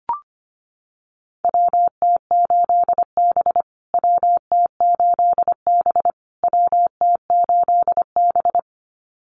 Morse code, heard as a keyed tone, about 25 wpm, 700 Hz. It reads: WT8B WT8B WT8B
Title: USA Call Signs Repeated 3x 25wpm